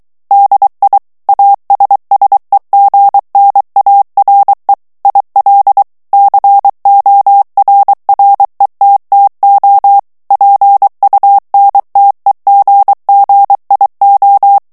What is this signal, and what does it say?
Morse code, heard as a keyed tone, about 23 wpm, 800 Hz. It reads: DI ASSEGNARE IL CORRETTO PUNTEGGIO
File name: ITALIEN texte clair 2 vitesse 22 mots minute